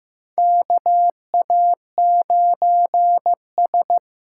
Morse code, heard as a keyed tone, about 15 wpm, 700 Hz. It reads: KA9S
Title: USA Call Signs - 6 Characters - 6x Character Spacing - 15wpm